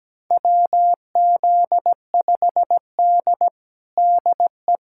Morse code, heard as a keyed tone, about 17 wpm, 700 Hz. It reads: WZ5D DE
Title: USA Callsign DE Callsign - 4 Characters - 17wpm